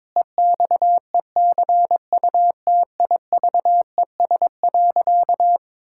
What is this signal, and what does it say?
Morse code, heard as a keyed tone, about 22 wpm, 700 Hz. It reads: EXECUTIVES.
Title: Pangram Challenge v2 - 22wpm